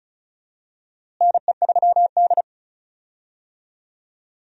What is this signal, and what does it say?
Morse code, heard as a keyed tone, about 36 wpm, 700 Hz. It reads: NE3D
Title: USA Call Signs - 4 Characters - Repeated 2x - 35wpm